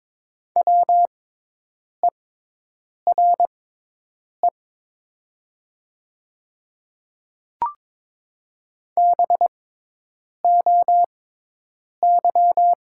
Morse code, heard as a keyed tone, about 22 wpm, spaced as for 8 wpm, 700 Hz. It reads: WERE BOY
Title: 6x Character Spacing - Top 200 Words - 22wpm